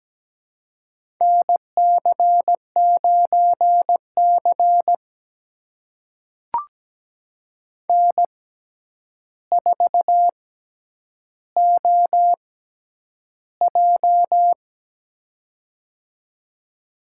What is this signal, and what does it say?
Morse code, heard as a keyed tone, about 17 wpm, spaced as for 9 wpm, 700 Hz. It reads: NC9C N4OJ
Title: USA Call Signs - 4 Characters - 6x Character Spacing - 17wpm